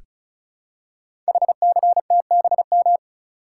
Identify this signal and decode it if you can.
Morse code, heard as a keyed tone, about 36 wpm, 700 Hz. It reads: HCTBM
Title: Code Groups - 5 Letters - 35wpm